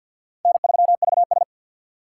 Morse code, heard as a keyed tone, about 53 wpm, 700 Hz. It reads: N3VS
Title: USA Call Signs - 4 Characters - 3x Character Spacing - 50wpm